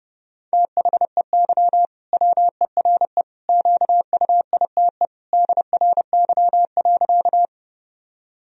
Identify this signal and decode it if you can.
Morse code, heard as a keyed tone, about 32 wpm, 700 Hz. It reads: THEY WERE QUITE DRY.
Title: Sentences from Top 1000 Words - Review Part 3 - 30wpm